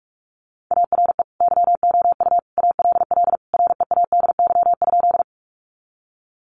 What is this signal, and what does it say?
Morse code, heard as a keyed tone, about 44 wpm, 700 Hz. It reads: ARE YOU ALL READY?
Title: Sentences from Top 400 Words 45wpm